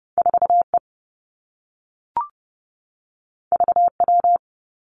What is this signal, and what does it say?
Morse code, heard as a keyed tone, about 30 wpm, 700 Hz. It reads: 4 E V W
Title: Two Challenging Letter-Number Combinations 30wpm